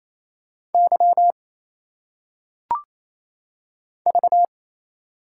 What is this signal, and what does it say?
Morse code, heard as a keyed tone, about 27 wpm, 700 Hz. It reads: Y V
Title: Single Letters - Phonetic Pronunciation 28wpm